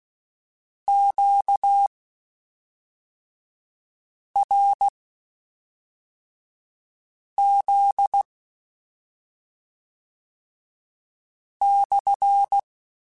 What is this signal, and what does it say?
Morse code, heard as a keyed tone, about 16 wpm, spaced as for 3 wpm, 800 Hz. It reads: QRZ/
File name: Q Calls / Q Calls-mp3s / Q Calls SPSL1 @5